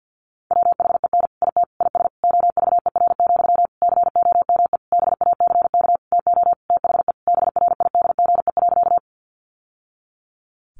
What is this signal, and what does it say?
Morse code, heard as a keyed tone, about 50 wpm, 700 Hz. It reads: WHEN IT IS OVER, COME BACK TO THE BRIDGE.
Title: Sentences from Top 1200 Words 50wpm